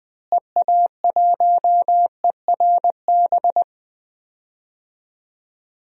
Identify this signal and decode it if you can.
Morse code, heard as a keyed tone, about 20 wpm, 700 Hz. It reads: EA1ERB
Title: DXCC-100 Call Signs - 6 Characters - Repeated 3x - 20wpm